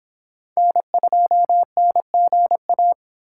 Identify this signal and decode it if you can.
Morse code, heard as a keyed tone, about 26 wpm, 700 Hz. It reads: N2NGA